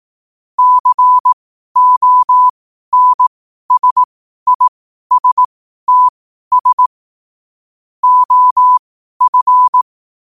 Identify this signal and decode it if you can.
Morse code, heard as a keyed tone, about 18 wpm, spaced as for 12 wpm, 1000 Hz. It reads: CONSISTS OF
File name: qotdfiles / 12wpm